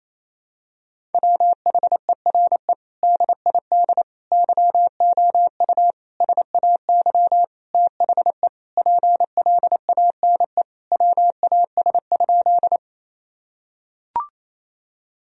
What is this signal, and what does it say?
Morse code, heard as a keyed tone, about 28 wpm, 700 Hz. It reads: WHERE DID YOU SAY THE PLANE WAS?
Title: Sentences from Top 1000 Words - Review Part 3 - 28wpm